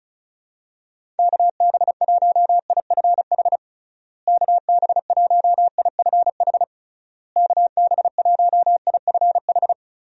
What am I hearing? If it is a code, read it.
Morse code, heard as a keyed tone, about 34 wpm, 700 Hz. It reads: KB1IFH KB1IFH KB1IFH
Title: USA Call Signs - 6 Characters - Repeated 3x - 35wpm